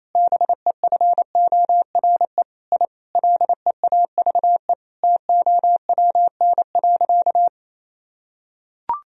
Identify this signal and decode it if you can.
Morse code, heard as a keyed tone, about 27 wpm, 700 Hz. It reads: BEFORE I LEAVE TOWN.
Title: Sentences from Top 1000 Words - Review Part 2 - 28wpm